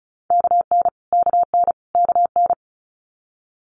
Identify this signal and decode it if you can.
Morse code, heard as a keyed tone, about 36 wpm, 700 Hz. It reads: KN KN KN
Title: Multi-letter Phonemes Repeated 3x 35wpm